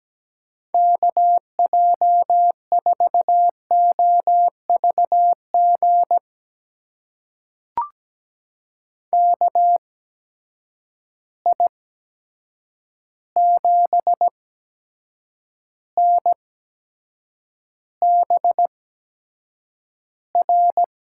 Morse code, heard as a keyed tone, about 17 wpm, spaced as for 7 wpm, 700 Hz. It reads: KJ4OVG KI7NBR
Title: USA Call Signs - 6 Characters - 8x Character Spacing - 17wpm